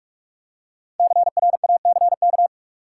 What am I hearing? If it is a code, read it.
Morse code, heard as a keyed tone, about 46 wpm, 700 Hz. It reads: CRACK
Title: Five Letter Words - 45wpm